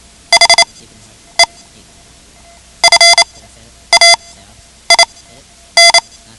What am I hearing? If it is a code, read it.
Morse code, heard as a keyed tone, about 28 wpm, spaced as for 10 wpm, 2050 Hz. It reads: HE FAIN
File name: ham radio / CodeQuick / Code Quick 10